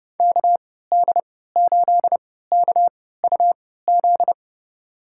Morse code, heard as a keyed tone, about 32 wpm, spaced as for 17 wpm, 700 Hz. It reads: KD8KUZ
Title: USA Call Signs - 6 Characters - 3x Character Spacing - 30wpm